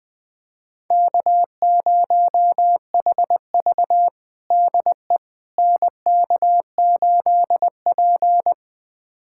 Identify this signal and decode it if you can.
Morse code, heard as a keyed tone, about 21 wpm, 700 Hz. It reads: K0HV DE NK8P
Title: USA Callsign DE Callsign - 4 Characters - 20wpm